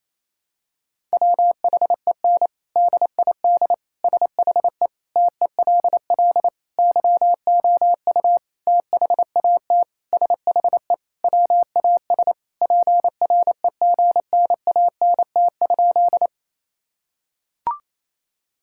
Morse code, heard as a keyed tone, about 28 wpm, 700 Hz. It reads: WHEN DID SHE TELL YOU THAT SHE WAS PREGNANT?
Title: Sentences from Top 1000 Words - Review Part 6 - 28wpm